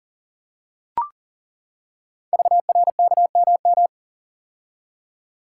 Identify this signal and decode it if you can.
Morse code, heard as a keyed tone, about 40 wpm, 700 Hz. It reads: VRKMM
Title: Code Groups - 5 Letters - 40wpm